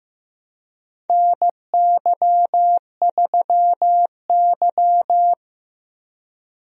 Morse code, heard as a keyed tone, about 15 wpm, 700 Hz. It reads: NY3Y
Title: USA Call Signs - 4 Characters - Repeated 3x - 15wpm